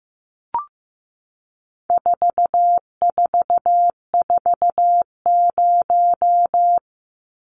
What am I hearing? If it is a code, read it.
Morse code, heard as a keyed tone, about 15 wpm, 700 Hz. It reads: 4440
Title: Numbers - 4-Digit 15wpm